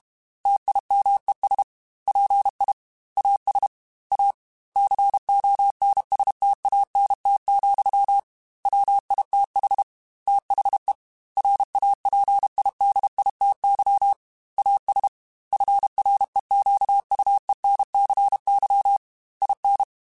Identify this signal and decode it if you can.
Morse code, heard as a keyed tone, about 32 wpm, 800 Hz. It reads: TIMES PI AS A CONSTANT, WITH THE RAPIDITY AS FREQUENCY IN
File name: Technical Subjects / Technical Subjects-mp3s / XL @30